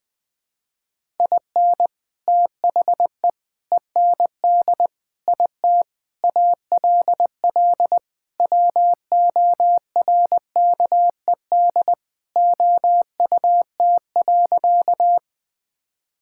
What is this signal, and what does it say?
Morse code, heard as a keyed tone, about 20 wpm, 700 Hz. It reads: IN THE END IT ALL WORKED OUT.
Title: Sentences with 3-Word Phrases - 20wpm